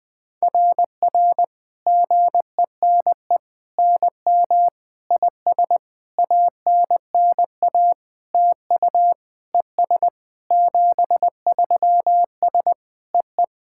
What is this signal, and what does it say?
Morse code, heard as a keyed tone, about 21 wpm, 690 Hz. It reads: RR GENE NM IS ANNA TU ES 73S EE
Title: SOTA QSO Practice 20wpm